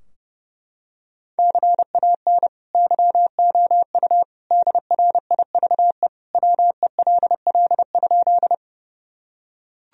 Morse code, heard as a keyed tone, about 30 wpm, 700 Hz. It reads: CAN YOU DRIVE WELL?
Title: Sentences from Top 1000 Words - Review Part 2 - 30wpm